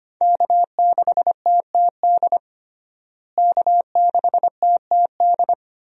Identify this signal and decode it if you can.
Morse code, heard as a keyed tone, about 25 wpm, 700 Hz. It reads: K6TTD K6TTD
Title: USA Call Signs - 5 Characters - Repeated 2x - 25wpm